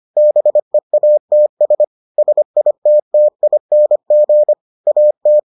Morse code, heard as a keyed tone, about 25 wpm, 600 Hz. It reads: BEATS SITTING AT